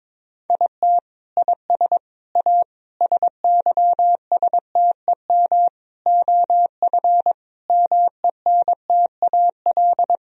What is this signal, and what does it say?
Morse code, heard as a keyed tone, about 22 wpm, 700 Hz. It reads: IT IS A SYSTEM OF MENTAL
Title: Sentences from Top 1000 Words - Review Part 4 - 22wpm